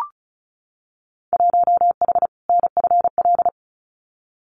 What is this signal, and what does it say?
Morse code, heard as a keyed tone, about 34 wpm, 690 Hz. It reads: 1H NFL
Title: Winter Field Day Exchanges 35wpm